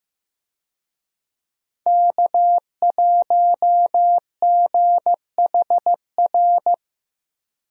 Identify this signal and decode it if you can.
Morse code, heard as a keyed tone, about 15 wpm, 700 Hz. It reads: K1GHR
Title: USA Call Signs - 5 Characters - 15wpm